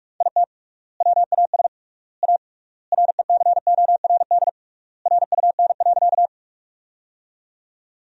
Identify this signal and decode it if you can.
Morse code, heard as a keyed tone, about 43 wpm, 700 Hz. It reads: IT WAS A RECORD RUN.
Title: 3x Word-Spacing - Sentences from Top 500 Words - 45wpm